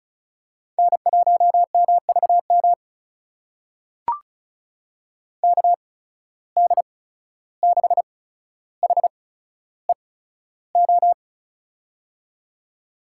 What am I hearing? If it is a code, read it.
Morse code, heard as a keyed tone, about 36 wpm, spaced as for 14 wpm, 700 Hz. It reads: N1MVM KD6HEO
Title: USA Call Signs - 8x Character Spacing - 35wpm